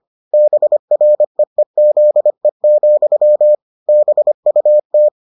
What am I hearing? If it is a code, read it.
Morse code, heard as a keyed tone, about 25 wpm, 600 Hz. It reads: BREEZE, BUT